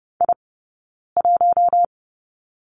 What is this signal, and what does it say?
Morse code, heard as a keyed tone, about 30 wpm, 700 Hz. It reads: I 1
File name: 092 Lesson 16d - Letter c - Two Characters 30wpm